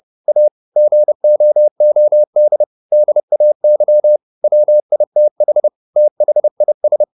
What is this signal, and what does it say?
Morse code, heard as a keyed tone, about 30 wpm, 600 Hz. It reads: A GOOD DAY WITH THIS
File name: the old man and the sea / the old man and the sea 1 30wpm